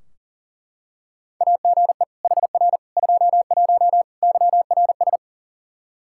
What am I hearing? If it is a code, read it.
Morse code, heard as a keyed tone, about 37 wpm, 700 Hz. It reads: AGE HR 21 YRS
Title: K1USN SST QSO Practice 40wpm